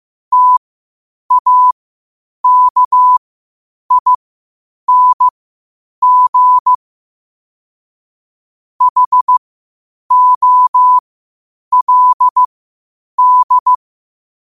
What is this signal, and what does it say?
Morse code, heard as a keyed tone, about 15 wpm, spaced as for 8 wpm, 1000 Hz. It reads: TAKING HOLD
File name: qotdfiles / 8wpm